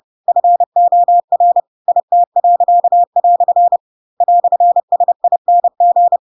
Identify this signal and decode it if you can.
Morse code, heard as a keyed tone, about 30 wpm, 700 Hz. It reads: FOR IT." "SING
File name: Pinocchio / Pinocchio ch4 30 30 ik1hge